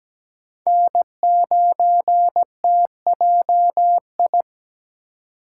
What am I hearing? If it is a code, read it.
Morse code, heard as a keyed tone, about 21 wpm, 700 Hz. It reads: N9TJI
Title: Speed Racing - USA Call Signs 17wpm